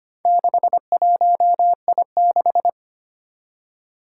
Morse code, heard as a keyed tone, about 25 wpm, 700 Hz. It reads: 61I6
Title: Four Challenging Letter-Number Combinations 25wpm